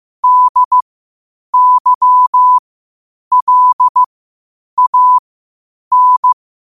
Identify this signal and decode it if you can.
Morse code, heard as a keyed tone, about 15 wpm, spaced as for 8 wpm, 1000 Hz. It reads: DYLAN
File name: qotdfiles / 8wpm